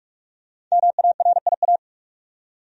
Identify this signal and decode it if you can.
Morse code, heard as a keyed tone, about 43 wpm, 700 Hz. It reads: MARIA
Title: USA Common Names - 45wpm